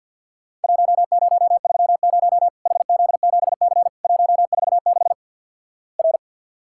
Morse code, heard as a keyed tone, about 50 wpm, 700 Hz. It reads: 1030 HZ7C 146 R
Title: Worked All Europe DX Contest - QTC Exchange with Repeats - 50wpm